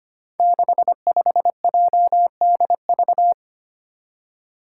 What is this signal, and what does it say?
Morse code, heard as a keyed tone, about 25 wpm, 700 Hz. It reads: 65JDV
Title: Five Challenging Letter-Number Combinations 25wpm